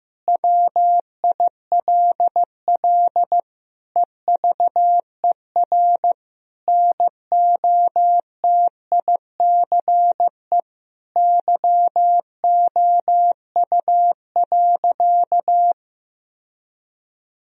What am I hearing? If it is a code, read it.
Morse code, heard as a keyed tone, about 16 wpm, 700 Hz. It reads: WILL EVER NOTICE YOU.
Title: Sentences from Top 1000 Words - Review Part 4 - 15wpm